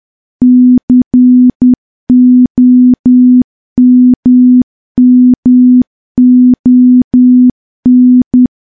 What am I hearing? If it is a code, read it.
Morse code, heard as a keyed tone, about 10 wpm, 250 Hz. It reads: COMMON
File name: morse / morse-training-10WPM